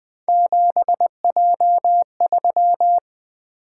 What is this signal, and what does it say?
Morse code, heard as a keyed tone, about 20 wpm, 700 Hz. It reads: 7J3
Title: Three Letter-Number Combinations - 20wpm